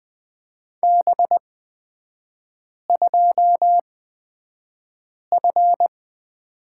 Morse code, heard as a keyed tone, about 20 wpm, spaced as for 5 wpm, 690 Hz. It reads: B2F
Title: 236 Lesson 38e - Letter q - Three Characters 20wpm